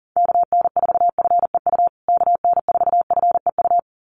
Morse code, heard as a keyed tone, about 40 wpm, 700 Hz. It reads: KN4FEU KN4FEU
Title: USA Call Signs Repeated 2x 40wpm